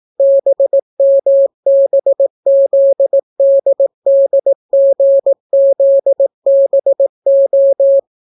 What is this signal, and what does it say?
Morse code, heard as a keyed tone, about 18 wpm, 550 Hz. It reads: BMBZDDGZBO